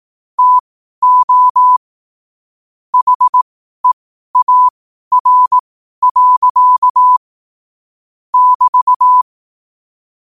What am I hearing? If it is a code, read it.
Morse code, heard as a keyed tone, about 18 wpm, spaced as for 12 wpm, 1000 Hz. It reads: TO HEAR. <BT>
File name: qotdfiles / 12wpm